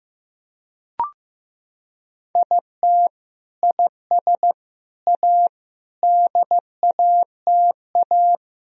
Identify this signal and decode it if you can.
Morse code, heard as a keyed tone, about 16 wpm, 700 Hz. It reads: IT IS A DATA
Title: Sentences from Top 1000 Words - Review Part 3 - 15wpm